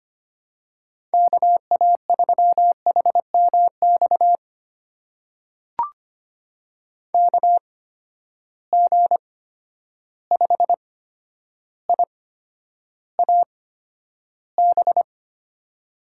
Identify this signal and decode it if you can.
Morse code, heard as a keyed tone, about 25 wpm, spaced as for 11 wpm, 700 Hz. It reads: KA3HMX KG5IAB
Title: USA Call Signs - 6 Characters - 8x Character Spacing - 25wpm